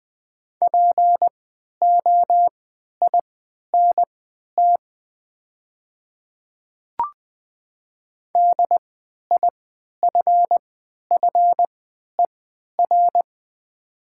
Morse code, heard as a keyed tone, about 20 wpm, spaced as for 11 wpm, 700 Hz. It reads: POINT DIFFER
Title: 3x Character Spacing - Top 200 Words - 20wpm